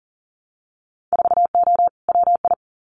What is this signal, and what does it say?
Morse code, heard as a keyed tone, about 39 wpm, 700 Hz. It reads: 4O WI
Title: Winter Field Day Exchanges 40wpm